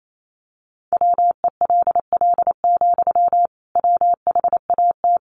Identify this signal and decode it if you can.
Morse code, heard as a keyed tone, about 28 wpm, 700 Hz. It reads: WELL, WHAT
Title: Sentences from Top 300 Words 28wpm